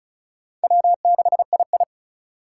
Morse code, heard as a keyed tone, about 36 wpm, 700 Hz. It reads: W6II
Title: USA Call Signs - 4 Characters - 2x Character Spacing - 35wpm